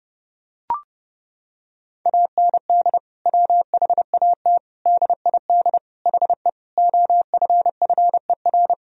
Morse code, heard as a keyed tone, about 29 wpm, 700 Hz. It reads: AND WHAT DID HE OFFER
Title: Sentences from Top 1000 Words - Review Part 4 - 30wpm